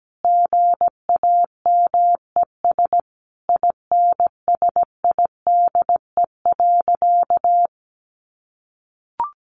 Morse code, heard as a keyed tone, about 17 wpm, 700 Hz. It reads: GAMES INSIDE.
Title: Easy Phrases 17wpm